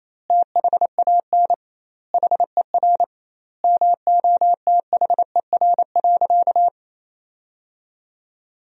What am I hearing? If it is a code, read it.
Morse code, heard as a keyed tone, about 28 wpm, 700 Hz. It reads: THAN HER MOTHER.
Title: 2x Word-Spacing - Sentences from Top 400 Words - 28wpm